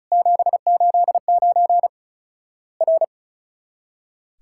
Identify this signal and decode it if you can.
Morse code, heard as a keyed tone, about 36 wpm, 700 Hz. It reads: 789 R
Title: Worked All Europe DX Contest - QTC Exchange - 35wpm